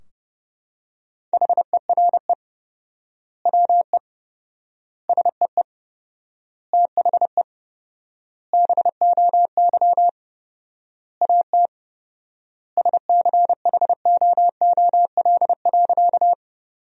Morse code, heard as a keyed tone, about 30 wpm, 700 Hz. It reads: HERE WE SEE THE BOY AT SCHOOL.
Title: 4x Word-Spacing - Sentences from Top 300 Words - 30wpm